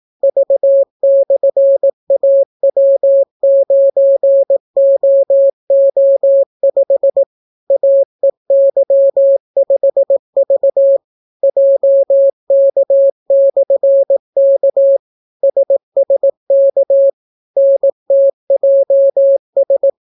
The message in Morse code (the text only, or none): V/AW9OO5 AEY5V JK/K SSK NTJS